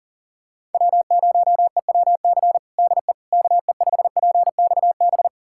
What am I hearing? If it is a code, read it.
Morse code, heard as a keyed tone, about 40 wpm, 700 Hz. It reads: W0EWC DE KE5PXB
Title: USA Callsign DE Callsign 40wpm